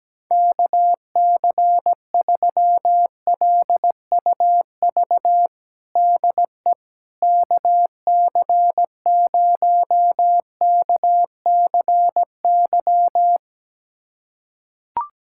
Morse code, heard as a keyed tone, about 17 wpm, 700 Hz. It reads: KC3LUV DE KC0KCY
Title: USA Callsign DE Callsign 17wpm